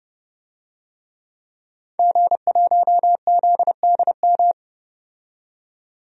Morse code, heard as a keyed tone, about 30 wpm, 700 Hz. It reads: G1ZDM
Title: DXCC-100 Call Signs - 5 Characters - 30wpm